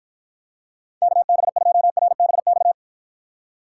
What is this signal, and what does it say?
Morse code, heard as a keyed tone, about 52 wpm, 700 Hz. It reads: KB2RBX